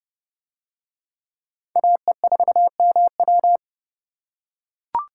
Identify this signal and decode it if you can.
Morse code, heard as a keyed tone, about 28 wpm, 700 Hz. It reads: AE4MW